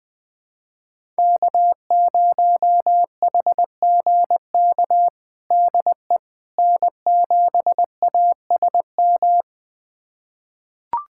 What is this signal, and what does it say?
Morse code, heard as a keyed tone, about 20 wpm, 700 Hz. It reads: K0HGK DE N7ASM
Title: USA Callsign DE Callsign - 5 Characters - 20wpm